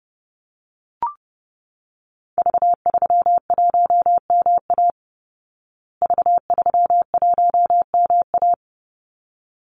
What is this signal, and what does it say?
Morse code, heard as a keyed tone, about 30 wpm, 700 Hz. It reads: V31MA V31MA
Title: Call Signs Repeated 3x - 30wpm